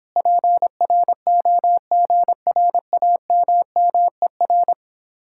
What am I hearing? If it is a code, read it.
Morse code, heard as a keyed tone, about 26 wpm, 700 Hz. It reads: PROGRAMMER
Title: Speed-Racing Common QSO Phrases Encore - 22wpm